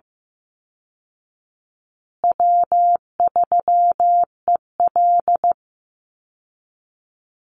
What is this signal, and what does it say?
Morse code, heard as a keyed tone, about 15 wpm, 700 Hz. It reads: W3EL
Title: Easy Call Signs Repeated 3x - 15wpm